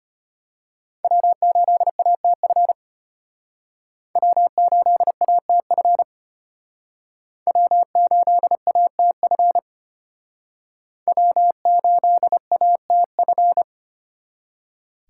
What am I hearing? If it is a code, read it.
Morse code, heard as a keyed tone, about 33 wpm, 700 Hz. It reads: W8ATF W8ATF W8ATF W8ATF